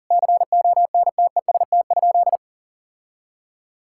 Morse code, heard as a keyed tone, about 40 wpm, 700 Hz. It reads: CONTEST?
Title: Sentences from Top 1600 Words 40wpm